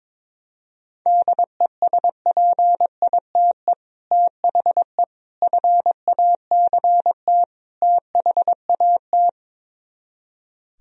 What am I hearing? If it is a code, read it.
Morse code, heard as a keyed tone, about 23 wpm, 700 Hz. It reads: DESPITE THE FACT THAT
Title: Speed-Racing - Sets of 4 Words 22wpm